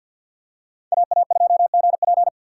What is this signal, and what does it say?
Morse code, heard as a keyed tone, about 52 wpm, 700 Hz. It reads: AA2GP
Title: USA Call Signs - 5 Characters - 3x Character Spacing - 50wpm